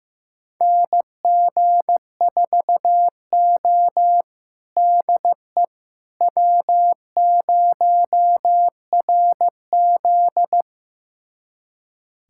Morse code, heard as a keyed tone, about 15 wpm, 700 Hz. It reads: NG4O DE W0RZ
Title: USA Callsign DE Callsign - 4 Characters - 15wpm